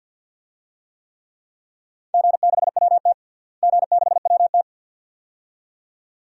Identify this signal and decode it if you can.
Morse code, heard as a keyed tone, about 54 wpm, 700 Hz. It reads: G6WT G6WT
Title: DXCC-100 Call Signs - 4 Characters - Repeated 2x - 50wpm